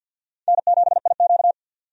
Morse code, heard as a keyed tone, about 52 wpm, 700 Hz. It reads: N7IQ
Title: USA Call Signs - 4 Characters - 50wpm